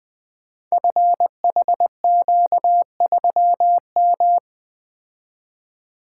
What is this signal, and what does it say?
Morse code, heard as a keyed tone, about 20 wpm, 700 Hz. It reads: FHQ3M
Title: Code Groups - 5 Letter:Numbers - 20wpm